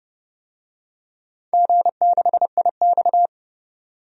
Morse code, heard as a keyed tone, about 30 wpm, 700 Hz. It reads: G6IX
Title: DXCC-100 Call Signs - 4 Characters - 30wpm